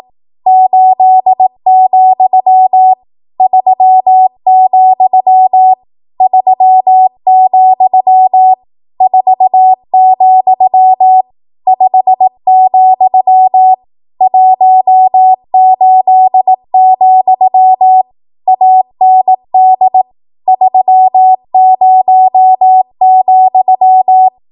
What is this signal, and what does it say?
Morse code, heard as a keyed tone, about 18 wpm, 750 Hz. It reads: 8, 3, 3, 4, 5, 18, AND 30,
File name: Archive / 18 WPM / 18WPM